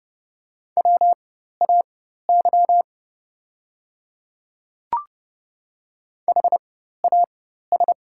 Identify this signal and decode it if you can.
Morse code, heard as a keyed tone, about 32 wpm, spaced as for 14 wpm, 700 Hz. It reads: WAY HAS
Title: 4x Character Spacing - Top 100 Words - 30wpm